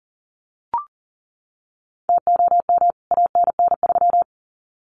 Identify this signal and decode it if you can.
Morse code, heard as a keyed tone, about 40 wpm, 700 Hz. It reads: TOM ANN3
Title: CWT Exchange 40wpm